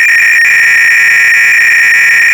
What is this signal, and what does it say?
300-baud modem data.
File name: ACR-Audio-Cassette-Programs / CharlesSwiger-Donations / TIL xyplot PDS-r